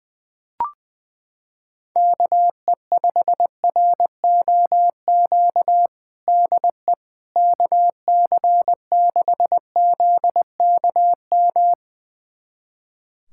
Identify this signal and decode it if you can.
Morse code, heard as a keyed tone, about 20 wpm, 700 Hz. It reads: KE5ROQ DE KC6ZKM
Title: USA Callsign DE Callsign - 6 Characters - 20wpm